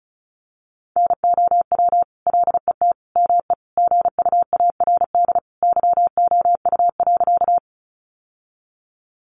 Morse code, heard as a keyed tone, about 34 wpm, 700 Hz. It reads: NOW LET ME GUARD YOU.
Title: Sentences from Top 700 Words 35wpm